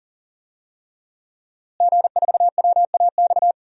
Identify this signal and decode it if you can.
Morse code, heard as a keyed tone, about 40 wpm, 700 Hz. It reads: G4WAX
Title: DXCC-100 Call Signs - 5 Characters - Repeated 3x - 40wpm